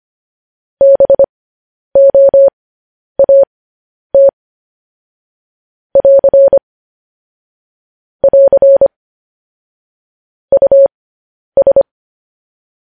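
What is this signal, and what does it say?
Morse code, heard as a keyed tone, about 25 wpm, spaced as for 10 wpm, 550 Hz. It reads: BOAT <AR> <AR> US
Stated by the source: Morse code news headlines broadcast.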